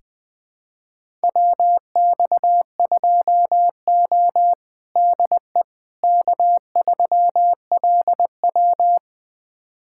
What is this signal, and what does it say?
Morse code, heard as a keyed tone, about 20 wpm, 700 Hz. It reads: WX2O DE K3LW
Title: USA Callsign DE Callsign - 4 Characters - 20wpm